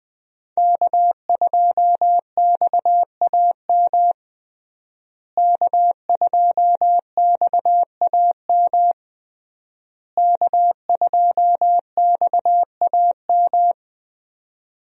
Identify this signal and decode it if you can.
Morse code, heard as a keyed tone, about 21 wpm, 700 Hz. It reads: K2XAM K2XAM K2XAM
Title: USA Call Signs - 5 Characters - Repeated 3x - 20wpm